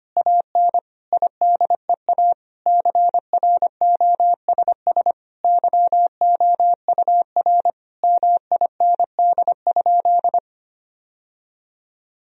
Morse code, heard as a keyed tone, about 25 wpm, 700 Hz. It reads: AN IDEA CROSS YOUR MIND?
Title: Sentences from Top 1000 Words - Review Part 1 - 25wpm